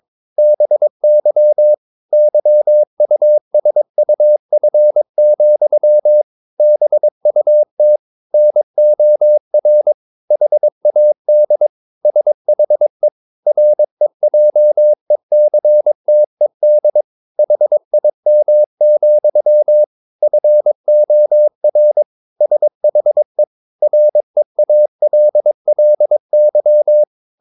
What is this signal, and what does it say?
Morse code, heard as a keyed tone, about 22 wpm, 600 Hz. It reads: BY YUSUF, BUT NOR HAD SHE REJECTED HIM, FOR SHE REALLY
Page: Romanen Never af Ken Follet på engelsk.